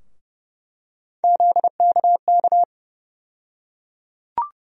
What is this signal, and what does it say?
Morse code, heard as a keyed tone, about 30 wpm, 700 Hz. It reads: ZKK
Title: USA Call Sign Postfixes - 3 Letters 30wpm